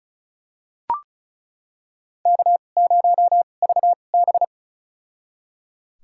Morse code, heard as a keyed tone, about 36 wpm, spaced as for 25 wpm, 700 Hz. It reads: K0VB
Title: USA Call Signs - 4 Characters - 2x Character Spacing - 35wpm